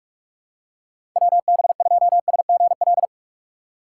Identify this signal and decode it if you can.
Morse code, heard as a keyed tone, about 45 wpm, 700 Hz. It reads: WB2SGL
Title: USA Call Signs - 6 Characters - 8x Character Spacing - 45wpm